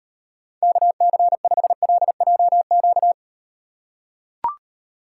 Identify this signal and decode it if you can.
Morse code, heard as a keyed tone, about 38 wpm, 700 Hz. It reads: KC5LJQ